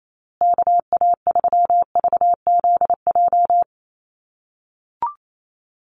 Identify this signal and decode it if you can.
Morse code, heard as a keyed tone, about 28 wpm, 690 Hz. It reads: KA3VZJ
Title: USA Call Signs Repeated 3x 28wpm